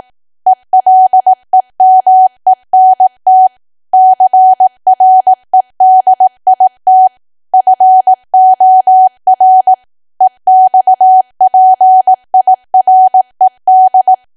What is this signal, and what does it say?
Morse code, heard as a keyed tone, about 18 wpm, 750 Hz. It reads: ELEMENT CREDIT FOR EXPIRED
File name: cwa / W1AW-Level2 / 18WPM